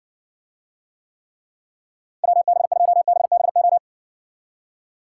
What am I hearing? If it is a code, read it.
Morse code, heard as a keyed tone, about 60 wpm, 700 Hz. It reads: WB2BBO